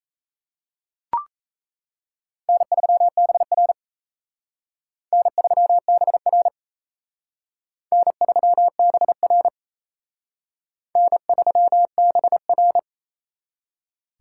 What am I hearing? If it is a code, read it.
Morse code, heard as a keyed tone, about 35 wpm, 700 Hz. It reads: N3BR N3BR N3BR N3BR
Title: Speed Racing - USA 1x2 Call Signs 28wpm